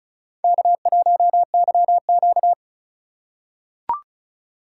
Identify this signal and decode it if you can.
Morse code, heard as a keyed tone, about 33 wpm, 700 Hz. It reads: K1YQ
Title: USA Call Signs - 4 Characters - 3x Character Spacing - 35wpm